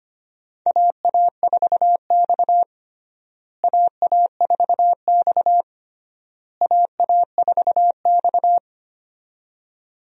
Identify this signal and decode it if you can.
Morse code, heard as a keyed tone, about 25 wpm, 700 Hz. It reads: AA4X AA4X AA4X
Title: USA Call Signs - 4 Characters - Repeated 3x - 25wpm